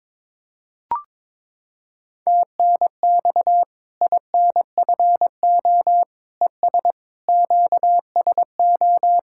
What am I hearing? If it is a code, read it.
Morse code, heard as a keyed tone, about 22 wpm, 700 Hz. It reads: TNX INFO ES QSO
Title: Practice QSO - Exchange 4 - Ending - 22wpm